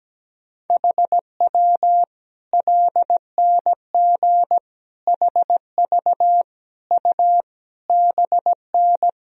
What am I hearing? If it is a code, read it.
Morse code, heard as a keyed tone, about 17 wpm, 700 Hz. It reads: HW LNG HV U BN
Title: Sentences from Top 200 Words and Abbreviations 17wpm